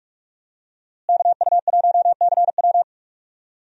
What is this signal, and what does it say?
Morse code, heard as a keyed tone, about 46 wpm, 700 Hz. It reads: KU1CW
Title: USA Call Signs - 5 Characters - Repeated 3x - 45wpm